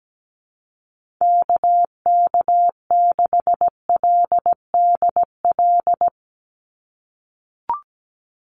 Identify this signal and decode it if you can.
Morse code, heard as a keyed tone, about 17 wpm, 700 Hz. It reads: KK6LDL
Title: USA Call Signs 17wpm